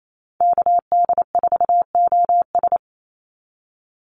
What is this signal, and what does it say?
Morse code, heard as a keyed tone, about 28 wpm, 700 Hz. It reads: KD4OS
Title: Call Signs - 28wpm